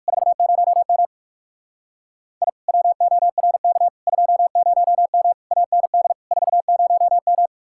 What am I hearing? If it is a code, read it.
Morse code, heard as a keyed tone, about 61 wpm, spaced as for 53 wpm, 700 Hz. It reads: 40M I WORK 20M AND 40M
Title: Speed-Racing Common QSO Phrases Encore - 45wpm